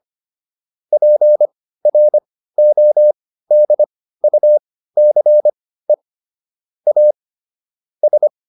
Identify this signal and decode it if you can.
Morse code, heard as a keyed tone, about 25 wpm, spaced as for 15 wpm, 600 Hz. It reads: PRODUCE A S
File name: SS215 15